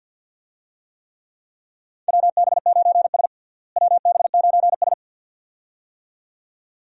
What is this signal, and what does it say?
Morse code, heard as a keyed tone, about 52 wpm, 700 Hz. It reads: WB9S WB9S
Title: USA Call Signs - 4 Characters - Repeated 2x - 50wpm